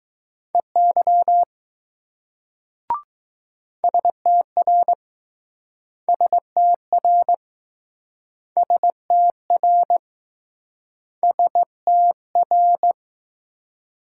Morse code, heard as a keyed tone, about 21 wpm, 700 Hz. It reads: EY STR STR STR STR